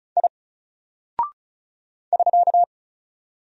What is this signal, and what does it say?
Morse code, heard as a keyed tone, about 34 wpm, 700 Hz. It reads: I <SK>
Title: Single Character (Letter:Number:Punctuation:Prosign) 35wpm